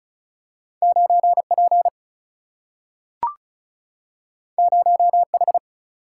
Morse code, heard as a keyed tone, about 34 wpm, 700 Hz. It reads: 9P 0H
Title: Single-Letter Single-Number 35wpm